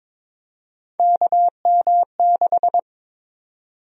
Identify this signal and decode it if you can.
Morse code, heard as a keyed tone, about 22 wpm, 700 Hz. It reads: KM6
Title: USA Call Sign Prefixes - 22wpm